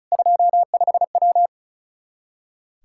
Morse code, heard as a keyed tone, about 34 wpm, 700 Hz. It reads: 25W
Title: K1USN SST QSO Practice 35wpm